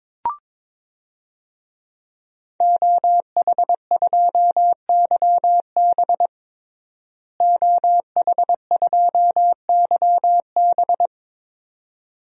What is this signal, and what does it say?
Morse code, heard as a keyed tone, about 22 wpm, 700 Hz. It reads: OH2YB OH2YB
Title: DXCC-100 Call Signs - 5 Characters - Repeated 2x - 22wpm